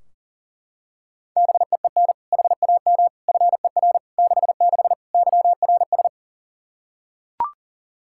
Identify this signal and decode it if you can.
Morse code, heard as a keyed tone, about 40 wpm, 700 Hz. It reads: BEEN HAM FER 66 YRS
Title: Practice QSO - Exchange 3 - More Info - 40wpm